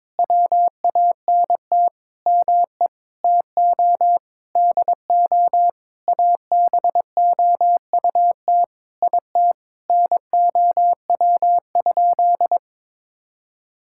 Morse code, heard as a keyed tone, about 22 wpm, 700 Hz. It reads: WANT ME TO DO ABOUT IT NOW?
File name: Sentences with 7-Word Phrases - 22wpm